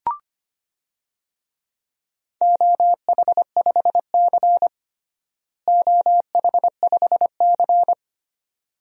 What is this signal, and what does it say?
Morse code, heard as a keyed tone, about 25 wpm, 700 Hz. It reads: OH5C OH5C
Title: DXCC-100 Call Signs - 4 Characters - Repeated 2x - 25wpm